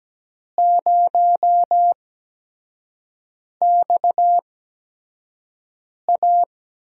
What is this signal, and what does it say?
Morse code, heard as a keyed tone, about 17 wpm, spaced as for 5 wpm, 700 Hz. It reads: 0XA
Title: USA Call Signs - 5 Characters - 8x Character Spacing - 17wpm